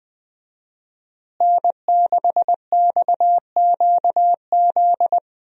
Morse code, heard as a keyed tone, about 20 wpm, 700 Hz. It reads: N6XQZ
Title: USA Call Signs - 5 Characters - Repeated 2x - 20wpm